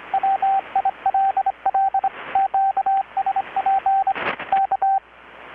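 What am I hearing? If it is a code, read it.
Morse code, heard as a keyed tone, about 23 wpm, 780 Hz, with radio noise. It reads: WILL QSP K
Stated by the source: Morse from KPH's Night of Nights XX maritime radiotelegraphy event, received in Chiapas, Mexico about 3,800 km away on a Kenwood R-600 with a grounded 10 m long-wire antenna.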